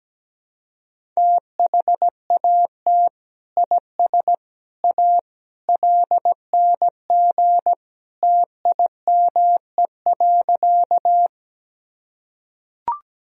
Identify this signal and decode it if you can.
Morse code, heard as a keyed tone, about 17 wpm, 700 Hz. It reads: THAT IS A LNG TIME.
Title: Sentences from Top 100 Words and Abbreviations 17wpm